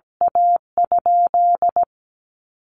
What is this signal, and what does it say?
Morse code, heard as a keyed tone, about 17 wpm, 700 Hz. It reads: A?
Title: Q-Codes Repeated 3x 17wpm